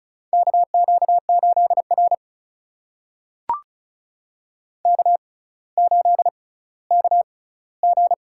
Morse code, heard as a keyed tone, about 36 wpm, spaced as for 18 wpm, 700 Hz. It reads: KQ8R K8KG
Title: USA Call Signs - 4 Characters - 6x Character Spacing - 35wpm